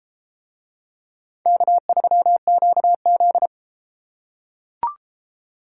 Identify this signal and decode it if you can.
Morse code, heard as a keyed tone, about 32 wpm, 700 Hz. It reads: K3QZ